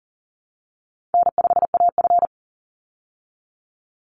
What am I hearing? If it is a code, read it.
Morse code, heard as a keyed tone, about 39 wpm, 700 Hz. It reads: N5AF
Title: Easy Call Signs Repeated 3x - 40wpm